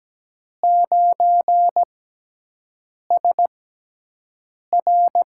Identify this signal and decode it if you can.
Morse code, heard as a keyed tone, about 17 wpm, spaced as for 6 wpm, 700 Hz. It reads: 9SR
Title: USA Call Signs - 6 Characters - 6x Character Spacing - 17wpm